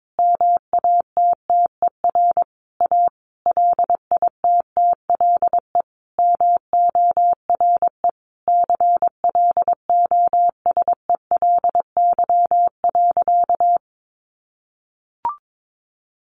Morse code, heard as a keyed tone, about 22 wpm, 700 Hz. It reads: MATTER A LITTLE MORE CLOSELY.
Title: Sentences from Top 1400 Words 22wpm